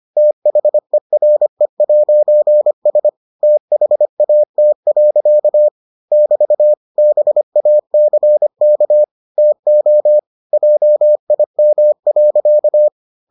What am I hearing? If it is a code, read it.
Morse code, heard as a keyed tone, about 25 wpm, 600 Hz. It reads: THERE'S THAT. <BT> BACK TO JIM.